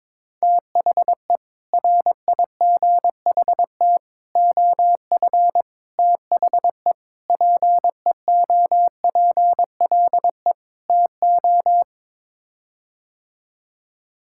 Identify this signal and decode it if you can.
Morse code, heard as a keyed tone, about 22 wpm, 700 Hz. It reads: THE RIGHT OF THE PEOPLE TO
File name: practice / 22wpm / Sets of 6 Words - 22wpm